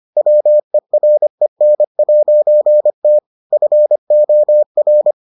Morse code, heard as a keyed tone, about 25 wpm, 600 Hz. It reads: WEREN'T FOR